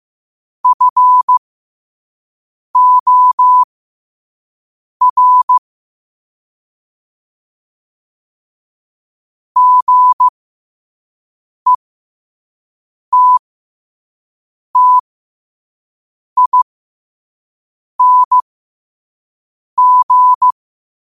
Morse code, heard as a keyed tone, about 15 wpm, spaced as for 5 wpm, 1000 Hz. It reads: FOR GETTING
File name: qotdfiles / 5wpm